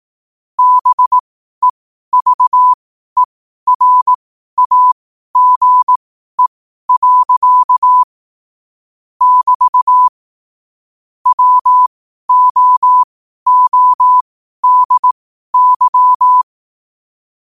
Morse code, heard as a keyed tone, about 18 wpm, spaced as for 12 wpm, 1000 Hz. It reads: BEVERAGE. <BT> WOODY
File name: qotdfiles / 12wpm